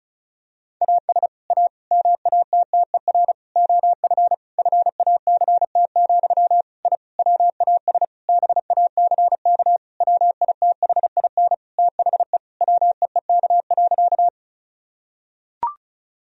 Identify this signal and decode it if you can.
Morse code, heard as a keyed tone, about 36 wpm, 700 Hz. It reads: AS A MATTER OF FACT, I WAS BACK WITHIN THE WEEK.
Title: Sentences from Top 1000 Words - Review Part 4 - 35wpm